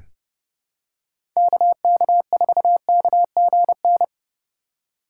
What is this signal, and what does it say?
Morse code, heard as a keyed tone, about 30 wpm, 700 Hz. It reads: KK4KGN
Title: USA Call Signs - 6 Characters - 3x Character Spacing - 30wpm